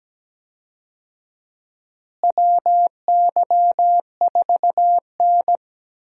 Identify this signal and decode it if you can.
Morse code, heard as a keyed tone, about 17 wpm, 700 Hz. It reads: WY4N
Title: USA Call Signs - 4 Characters - Repeated 2x - 17wpm